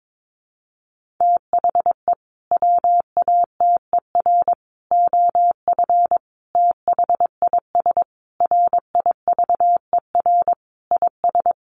Morse code, heard as a keyed tone, about 22 wpm, 700 Hz. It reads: THE WATER OF THIS RIVER IS
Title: Sentences from Top 1500 Words 22wpm